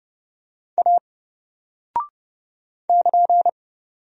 Morse code, heard as a keyed tone, about 31 wpm, 700 Hz. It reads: A <KN>
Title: Single Character (Letter:Number:Punctuation:Prosign) 30wpm